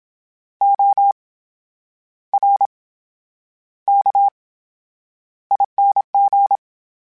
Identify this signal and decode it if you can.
Morse code, heard as a keyed tone, about 27 wpm, spaced as for 9 wpm, 800 Hz. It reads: ORKING
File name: Special INGs / Special INGs-mp3s / VerySlowToFaster Ing @25